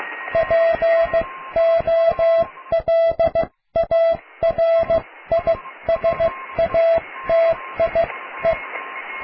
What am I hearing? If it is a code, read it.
Morse code, heard as a keyed tone, about 15 wpm, spaced as for 13 wpm, 660 Hz. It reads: POLARISATIE
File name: cw / tekst(a)+QRM